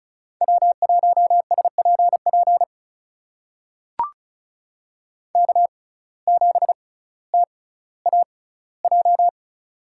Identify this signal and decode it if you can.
Morse code, heard as a keyed tone, about 36 wpm, spaced as for 18 wpm, 700 Hz. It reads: W1SPP K7TAJ
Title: USA Call Signs - 5 Characters - 6x Character Spacing - 35wpm